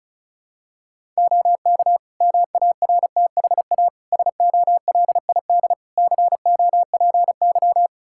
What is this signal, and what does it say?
Morse code, heard as a keyed tone, about 36 wpm, 700 Hz. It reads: OK MARTHA SOLID COPY
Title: Practice QSO - Exchange 3 - More Info - 35wpm